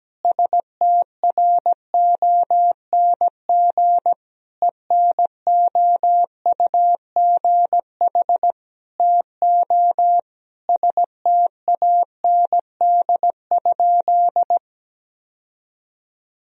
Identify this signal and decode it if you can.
Morse code, heard as a keyed tone, about 17 wpm, 700 Hz. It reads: STRONG ENOUGH TO STAND?
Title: Sentences from Top 1000 Words - Review Part 3 - 17wpm